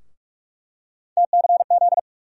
Morse code, heard as a keyed tone, about 47 wpm, 700 Hz. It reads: TCZ
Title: USA Call Sign Postfixes - 3 Letters 45wpm